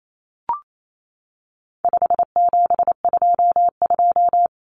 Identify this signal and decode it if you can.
Morse code, heard as a keyed tone, about 28 wpm, 700 Hz. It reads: 5722
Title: Common Cut Numbers - 4-Digit 28wpm